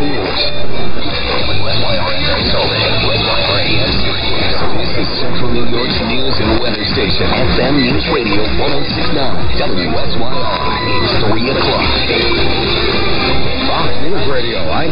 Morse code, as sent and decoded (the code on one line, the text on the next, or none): none